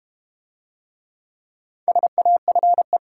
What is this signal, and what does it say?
Morse code, heard as a keyed tone, about 31 wpm, 690 Hz. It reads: SAFE
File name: Easy Words - Speed-Racing with Hints - 25wpm